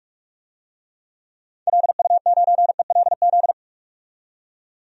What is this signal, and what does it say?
Morse code, heard as a keyed tone, about 44 wpm, 700 Hz. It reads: LU9ELZ
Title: DXCC-100 Call Signs - 6 Characters - 45wpm